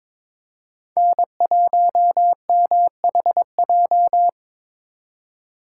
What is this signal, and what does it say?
Morse code, heard as a keyed tone, about 26 wpm, 700 Hz. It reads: N1MHJ
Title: Speed Racing - USA Call Signs 22wpm